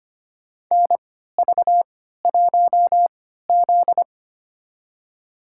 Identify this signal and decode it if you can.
Morse code, heard as a keyed tone, about 25 wpm, spaced as for 14 wpm, 700 Hz. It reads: NV1Z
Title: USA Call Signs - 3x Character Spacing - 25wpm